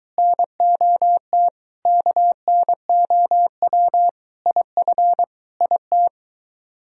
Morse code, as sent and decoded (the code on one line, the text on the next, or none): -. --- - / -.- -. --- .-- / .. ..-. / .. -
NOT KNOW IF IT